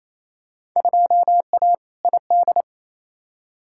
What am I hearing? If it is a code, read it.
Morse code, heard as a keyed tone, about 28 wpm, 700 Hz. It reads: 2A ID
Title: Field Day Exchanges Encore - 28wpm